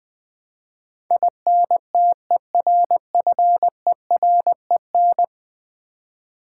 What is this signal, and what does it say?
Morse code, heard as a keyed tone, about 20 wpm, 700 Hz. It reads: INTERFEREN
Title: Build A Word - 5 Letters At A Time - 12-Letter Words - 20wpm